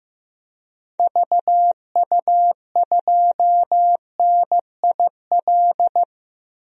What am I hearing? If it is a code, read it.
Morse code, heard as a keyed tone, about 15 wpm, 700 Hz. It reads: VU2NIL
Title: DXCC-100 Call Signs - 6 Characters - Repeated 2x - 15wpm